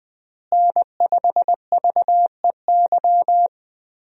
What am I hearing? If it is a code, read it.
Morse code, heard as a keyed tone, about 20 wpm, 700 Hz. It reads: N5VEY
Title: USA Call Signs - 5 Characters - Repeated 3x - 20wpm